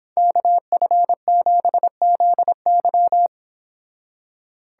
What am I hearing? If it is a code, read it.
Morse code, heard as a keyed tone, about 30 wpm, 700 Hz. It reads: KF7ZY